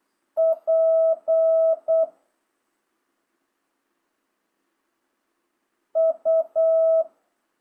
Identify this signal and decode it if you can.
Morse code, heard as a keyed tone, about 8 wpm, 650 Hz. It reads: P U